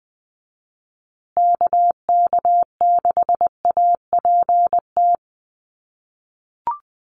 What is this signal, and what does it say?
Morse code, heard as a keyed tone, about 20 wpm, 700 Hz. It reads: KK6APT
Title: USA Call Signs Repeated 2x 20wpm